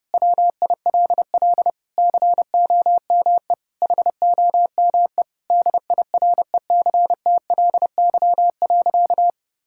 Morse code, heard as a keyed tone, about 30 wpm, 700 Hz. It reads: WILL COME HOME DIRECTLY.
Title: Sentences from Top 1000 Words - Review Part 5 - 30wpm